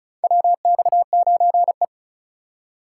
Morse code, heard as a keyed tone, about 36 wpm, 700 Hz. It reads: WX9E
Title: USA Call Signs - 4x Character Spacing - 35wpm